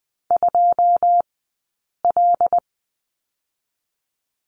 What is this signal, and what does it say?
Morse code, heard as a keyed tone, about 20 wpm, spaced as for 8 wpm, 700 Hz. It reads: 2L
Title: 080 Lesson 14d - Number 2 - Two Characters 20wpm